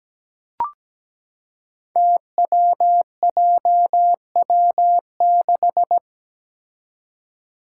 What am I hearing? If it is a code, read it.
Morse code, heard as a keyed tone, about 17 wpm, 700 Hz. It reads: TWJW6
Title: Five Challenging Letter-Number Combinations 17wpm